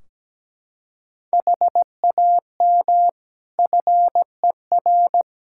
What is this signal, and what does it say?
Morse code, heard as a keyed tone, about 17 wpm, 700 Hz. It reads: HAM FER
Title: K1USN SST QSO Practice 17wpm